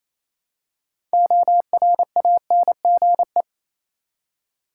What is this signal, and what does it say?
Morse code, heard as a keyed tone, about 28 wpm, 700 Hz. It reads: ORANGE
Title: Six Letter Words - 28wpm